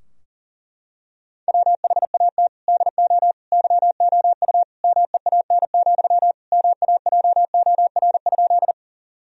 Morse code, heard as a keyed tone, about 40 wpm, 700 Hz. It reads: WHAT DO YOU MEAN, MAJOR?
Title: Sentences from Top 1000 Words - Review Part 4 - 40wpm